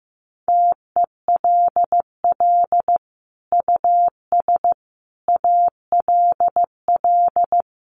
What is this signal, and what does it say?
Morse code, heard as a keyed tone, about 15 wpm, 700 Hz. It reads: TELL US ALL
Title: Sentences from Top 200 Words 15wpm